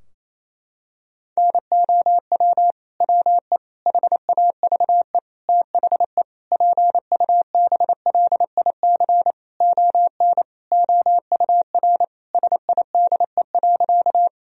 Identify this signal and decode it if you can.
Morse code, heard as a keyed tone, about 28 wpm, 700 Hz. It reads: NOW WE HAVE THE PUBLIC ON OUR SIDE.
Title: Sentences from Top 1000 Words - Review Part 3 - 28wpm